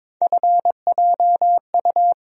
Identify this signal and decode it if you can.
Morse code, heard as a keyed tone, about 22 wpm, 700 Hz. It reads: FJU
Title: USA Call Sign Postfixes - 3 Letters 22wpm